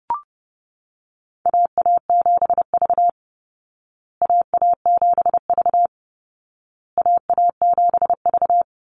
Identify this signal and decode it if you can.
Morse code, heard as a keyed tone, about 30 wpm, 700 Hz. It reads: AA7V AA7V AA7V
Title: Call Signs Repeated 2x - 30wpm